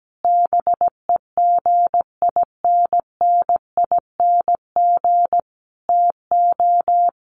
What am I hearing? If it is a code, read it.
Morse code, heard as a keyed tone, about 17 wpm, 700 Hz. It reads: BEGINNING TO
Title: Sentences from Top 1200 Words 17wpm